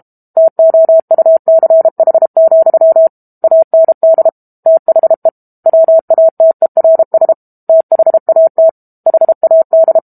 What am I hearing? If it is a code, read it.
Morse code, heard as a keyed tone, about 32 wpm, 650 Hz. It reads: TOUCH, AND THE WATERS THAT HAD
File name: Book Two / 06-Chapter 6 - 32 WPM